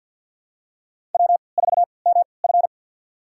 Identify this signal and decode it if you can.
Morse code, heard as a keyed tone, about 50 wpm, spaced as for 28 wpm, 700 Hz. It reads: W4MF
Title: USA Call Signs - 4 Characters - 3x Character Spacing - 50wpm